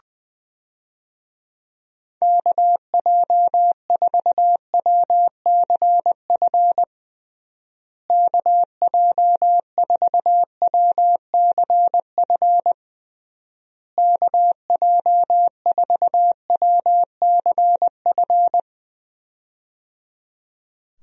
Morse code, heard as a keyed tone, about 20 wpm, 700 Hz. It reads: KJ4WCF KJ4WCF KJ4WCF
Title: USA Call Signs - 6 Characters - Repeated 3x - 20wpm